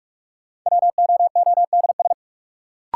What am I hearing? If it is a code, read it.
Morse code, heard as a keyed tone, about 45 wpm, 700 Hz. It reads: WOODS
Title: Sentences from Top 1000 Words - Review Part 2 - New Word Drill - 45wpm